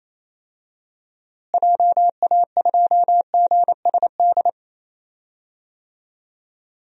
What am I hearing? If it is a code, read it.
Morse code, heard as a keyed tone, about 28 wpm, 700 Hz. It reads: JA2GSD
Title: DXCC-100 Call Signs - 6 Characters - Repeated 2x - 28wpm